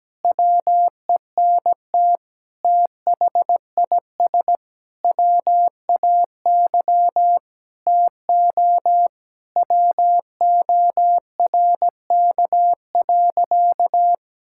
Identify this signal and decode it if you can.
Morse code, heard as a keyed tone, about 17 wpm, 700 Hz. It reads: WENT THIS WAY TO WORK.
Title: Sentences from Top 1000 Words - Review Part 5 - 17wpm